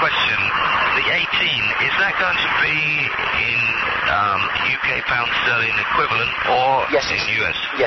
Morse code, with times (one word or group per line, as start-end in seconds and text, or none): none